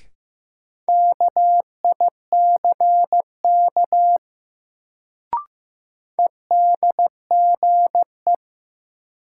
Morse code, heard as a keyed tone, about 16 wpm, 700 Hz. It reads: KICK EDGE
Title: Sentences from Top 1300 Words - New Word Drill Alt - 15wpm